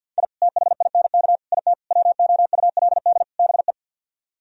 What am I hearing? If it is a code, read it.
Morse code, heard as a keyed tone, about 65 wpm, spaced as for 57 wpm, 690 Hz. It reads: I THINK IT WOULD BE
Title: Speed-Racing - Sets of 5 Words 50wpm